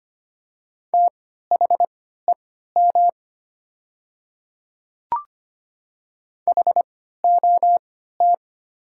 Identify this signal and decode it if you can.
Morse code, heard as a keyed tone, about 25 wpm, spaced as for 14 wpm, 700 Hz. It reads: THEM HOT
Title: 3x Character Spacing - Top 300 Words - 25wpm